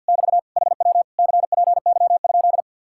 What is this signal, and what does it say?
Morse code, heard as a keyed tone, about 62 wpm, 700 Hz. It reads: <BT> HW CPY?
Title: Speed-Racing Common QSO Phrases - 50wpm